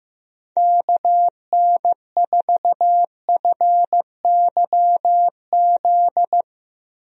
Morse code, heard as a keyed tone, about 15 wpm, 700 Hz. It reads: KN4FYZ
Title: USA Call Signs - 6 Characters - 15wpm